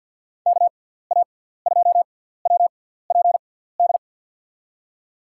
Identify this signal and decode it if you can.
Morse code, heard as a keyed tone, about 50 wpm, spaced as for 17 wpm, 700 Hz. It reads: KA2WPD
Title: USA Call Signs - 6x Character Spacing - 50wpm